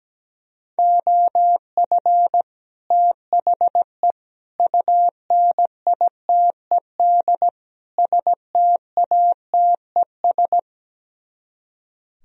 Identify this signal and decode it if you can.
Morse code, heard as a keyed tone, about 17 wpm, 700 Hz. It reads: OF THE UNITED STATES
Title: Speed-Racing - Sets of 4 Words 17wpm